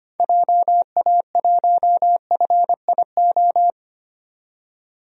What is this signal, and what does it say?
Morse code, heard as a keyed tone, about 25 wpm, 690 Hz. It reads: JA1FIO
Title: DXCC-100 Call Signs - 6 Characters - 25wpm